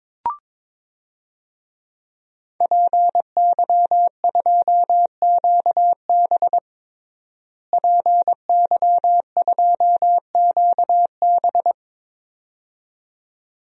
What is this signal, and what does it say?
Morse code, heard as a keyed tone, about 22 wpm, 700 Hz. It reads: PY2QB PY2QB
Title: DXCC-100 Call Signs - 5 Characters - Repeated 2x - 22wpm